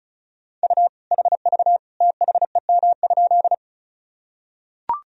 Morse code, heard as a keyed tone, about 34 wpm, 700 Hz. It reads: U HV THEM?
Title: Sentences from Top 100 Words and Abbreviations 35wpm